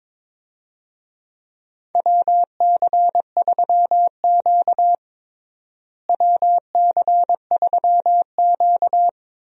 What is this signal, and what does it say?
Morse code, heard as a keyed tone, about 22 wpm, 700 Hz. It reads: WC3Q WC3Q
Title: USA Call Signs - 4 Characters - Repeated 2x - 22wpm